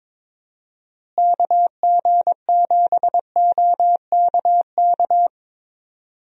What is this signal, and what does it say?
Morse code, heard as a keyed tone, about 22 wpm, 700 Hz. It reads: KG7OKK
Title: USA Call Signs - 6 Characters - Repeated 2x - 22wpm